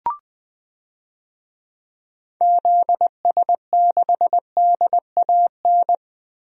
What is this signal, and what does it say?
Morse code, heard as a keyed tone, about 20 wpm, 700 Hz. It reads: ZS6DAN
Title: DXCC-100 Call Signs - 6 Characters - Repeated 3x - 20wpm